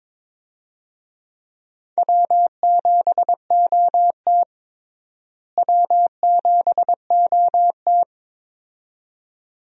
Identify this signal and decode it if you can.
Morse code, heard as a keyed tone, about 22 wpm, 700 Hz. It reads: W7OT W7OT
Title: USA Call Signs - 4 Characters - Repeated 2x - 22wpm